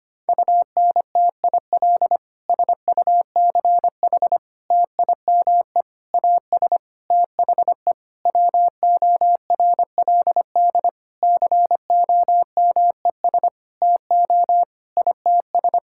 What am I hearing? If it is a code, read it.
Morse code, heard as a keyed tone, about 25 wpm, 700 Hz. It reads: UNTIL SUCH TIME AS THE WORLD COMES TO ITS
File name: practice / 25wpm / Sentences with 9-Word Phrases - 25wpm